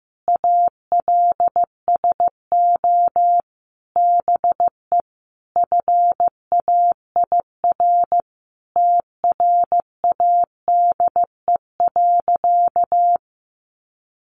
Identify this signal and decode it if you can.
Morse code, heard as a keyed tone, about 15 wpm, 700 Hz. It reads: ALSO BE FAIR TRADE.
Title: Sentences from Top 1300 Words 15wpm